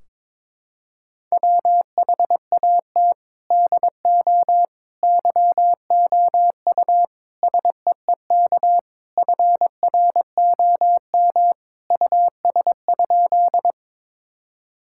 Morse code, heard as a keyed tone, about 21 wpm, 700 Hz. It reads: WHAT DO YOU SEEK FROM US?
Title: Sentences from Top 1600 Words 22wpm